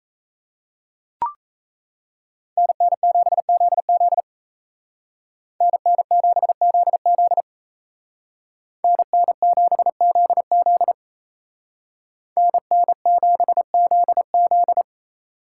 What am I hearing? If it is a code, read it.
Morse code, heard as a keyed tone, about 33 wpm, 700 Hz. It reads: NN7ZZ NN7ZZ NN7ZZ NN7ZZ
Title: Speed Racing - USA 2x2 Call Signs 28wpm